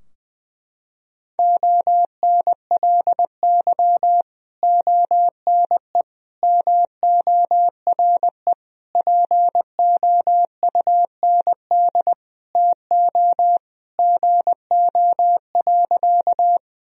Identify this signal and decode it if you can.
Morse code, heard as a keyed tone, about 21 wpm, 700 Hz. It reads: ONLY ONE MORE POUND TO GO.
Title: Sentences from Top 1000 Words - Review Part 2 - 20wpm